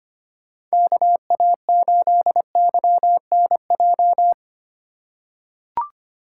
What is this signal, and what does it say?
Morse code, heard as a keyed tone, about 25 wpm, 700 Hz. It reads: KA8YNJ
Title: USA Call Signs - 3x Character Spacing - 25wpm